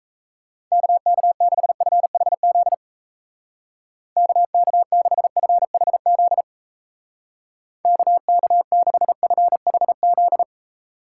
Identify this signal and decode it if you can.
Morse code, heard as a keyed tone, about 36 wpm, 700 Hz. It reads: KK6FHZ KK6FHZ KK6FHZ